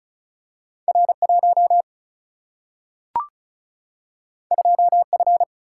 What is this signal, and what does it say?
Morse code, heard as a keyed tone, about 36 wpm, 700 Hz. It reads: R 1 2 F
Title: Single-Letter Single-Number 35wpm